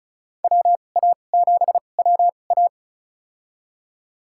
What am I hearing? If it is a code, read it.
Morse code, heard as a keyed tone, about 34 wpm, spaced as for 25 wpm, 700 Hz. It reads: WA7WA
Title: USA Call Signs - 5 Characters - 2x Character Spacing - 35wpm